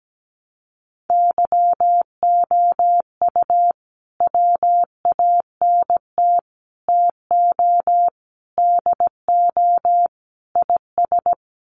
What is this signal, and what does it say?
Morse code, heard as a keyed tone, about 17 wpm, 700 Hz. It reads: YOU WANT TO DO IS
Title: Sets of 5 Words Encore2 17wpm